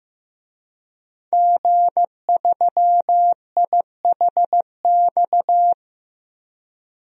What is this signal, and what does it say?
Morse code, heard as a keyed tone, about 15 wpm, 700 Hz. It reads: G3IHX
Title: DXCC-100 Call Signs - 5 Characters - 15wpm